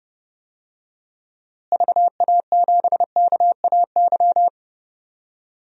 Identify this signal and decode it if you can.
Morse code, heard as a keyed tone, about 30 wpm, 700 Hz. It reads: VA7KAY
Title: DXCC-100 Call Signs - 6 Characters - Repeated 2x - 30wpm